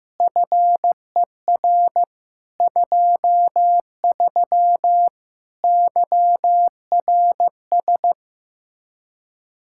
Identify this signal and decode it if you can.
Morse code, heard as a keyed tone, about 15 wpm, 700 Hz. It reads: FER 23 YRS
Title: Speed-Racing Common QSO Phrases Encore - 15wpm